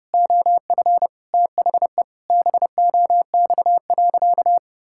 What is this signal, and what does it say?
Morse code, heard as a keyed tone, about 30 wpm, 700 Hz. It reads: OF THE BOX.
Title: Sentences from Top 1000 Words - Review Part 6 - 30wpm